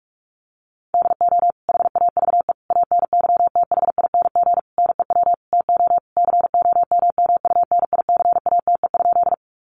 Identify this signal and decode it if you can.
Morse code, heard as a keyed tone, about 44 wpm, 700 Hz. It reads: DO HAVE ANYTHING NEW TO COMMUNICATE?
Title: Sentences from Top 1300 Words 45wpm